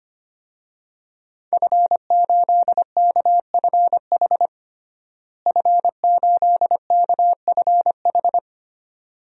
Morse code, heard as a keyed tone, about 25 wpm, 700 Hz. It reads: F8KFH F8KFH
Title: DXCC-100 Call Signs - 5 Characters - Repeated 2x - 25wpm